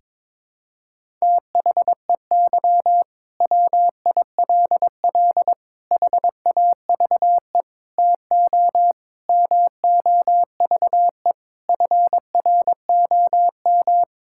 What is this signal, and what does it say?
Morse code, heard as a keyed tone, about 21 wpm, 700 Hz. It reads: THEY WILL HAVE TO MOVE FROM
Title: Sentences from Top 1000 Words - Review Part 6 - 22wpm